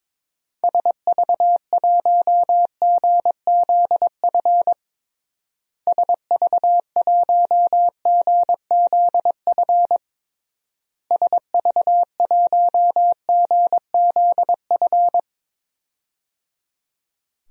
Morse code, heard as a keyed tone, about 22 wpm, 700 Hz. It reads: SV1GZF SV1GZF SV1GZF
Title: DXCC-100 Call Signs - 6 Characters - Repeated 3x - 22wpm